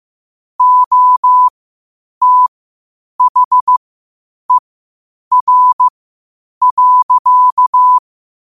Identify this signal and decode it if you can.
Morse code, heard as a keyed tone, about 15 wpm, spaced as for 8 wpm, 1000 Hz. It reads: OTHER.
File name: qotdfiles / 8wpm